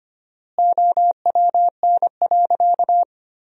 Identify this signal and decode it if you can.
Morse code, heard as a keyed tone, about 25 wpm, 700 Hz. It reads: OWN.
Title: 4x Word-Spacing - Sentences from Top 300 Words - 25wpm